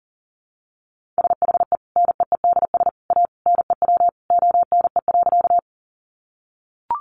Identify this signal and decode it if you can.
Morse code, heard as a keyed tone, about 40 wpm, 700 Hz. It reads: SHE NEEDS A NEW ONE.
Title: Additional Easy Phrases 40wpm